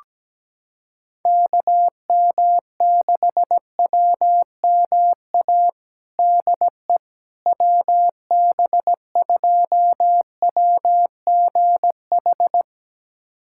Morse code, heard as a keyed tone, about 17 wpm, 700 Hz. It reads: KM6WMA DE WB2WGH
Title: USA Callsign DE Callsign - 6 Characters - 17wpm